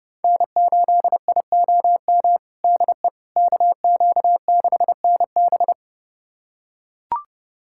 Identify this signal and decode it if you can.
Morse code, heard as a keyed tone, about 30 wpm, 700 Hz. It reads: N8IOM DE KQ6NB
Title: USA Callsign DE Callsign - 5 Characters - 30wpm